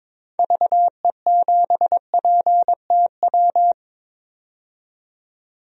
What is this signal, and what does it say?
Morse code, heard as a keyed tone, about 22 wpm, 700 Hz. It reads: VE7PTW
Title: DXCC-100 Call Signs - 6 Characters - 22wpm